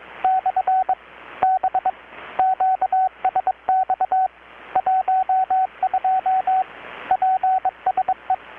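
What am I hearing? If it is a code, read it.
Morse code, heard as a keyed tone, about 22 wpm, 750 Hz, with radio noise. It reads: / B QSX 12 PSE
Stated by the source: Morse from KPH's Night of Nights XX maritime radiotelegraphy event, received in Chiapas, Mexico about 3,800 km away on a Kenwood R-600 with a grounded 10 m long-wire antenna.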